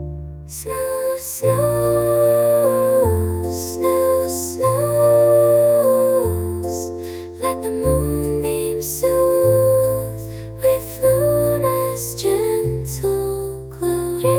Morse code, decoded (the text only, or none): none